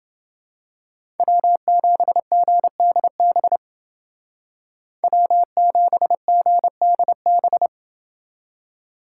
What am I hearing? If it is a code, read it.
Morse code, heard as a keyed tone, about 29 wpm, 700 Hz. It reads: W7GDB W7GDB